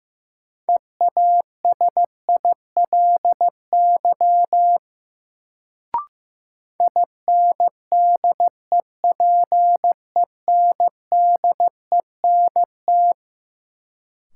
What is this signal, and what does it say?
Morse code, heard as a keyed tone, about 15 wpm, 700 Hz. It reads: EASILY INDEPENDENT
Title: Sentences from Top 900 Words - New Word Drill - 15wpm